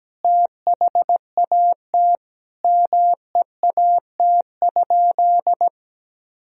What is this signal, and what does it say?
Morse code, heard as a keyed tone, about 17 wpm, 700 Hz. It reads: THAT MEAT?
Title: Sentences from Top 1000 Words - Review Part 3 - 17wpm